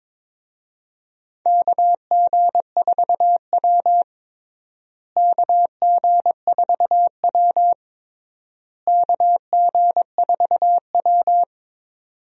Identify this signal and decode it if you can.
Morse code, heard as a keyed tone, about 22 wpm, 700 Hz. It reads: KG4W KG4W KG4W
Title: USA Call Signs - 4 Characters - Repeated 3x - 22wpm